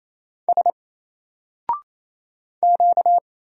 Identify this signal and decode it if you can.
Morse code, heard as a keyed tone, about 27 wpm, 700 Hz. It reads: S Q
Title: Single Letter-Number 28wpm